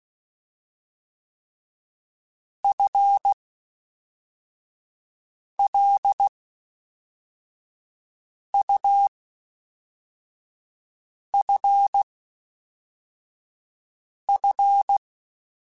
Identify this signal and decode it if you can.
Morse code, heard as a keyed tone, about 16 wpm, spaced as for 4 wpm, 800 Hz. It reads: FLUFF
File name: Spaced / FL Runs sp mp3s / FL Runs sp @5